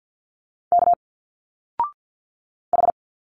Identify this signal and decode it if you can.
Morse code, heard as a keyed tone, about 49 wpm, 700 Hz. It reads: K H
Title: Single Letters 50wpm